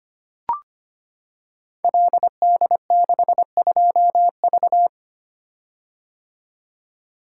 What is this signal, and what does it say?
Morse code, heard as a keyed tone, about 25 wpm, 700 Hz. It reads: LD62V
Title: Code Groups - 5 Letter:Numbers - 25wpm